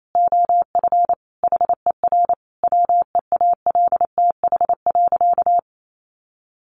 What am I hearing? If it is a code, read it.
Morse code, heard as a keyed tone, about 28 wpm, 700 Hz. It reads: OF HER WEALTH.
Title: Sentences from Top 1200 Words 28wpm